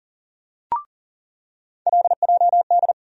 Morse code, heard as a keyed tone, about 40 wpm, 700 Hz. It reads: LJD
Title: USA Call Sign Postfixes - 3 Letters 40wpm